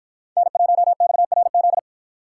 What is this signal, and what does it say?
Morse code, heard as a keyed tone, about 53 wpm, 700 Hz. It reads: N1XRZ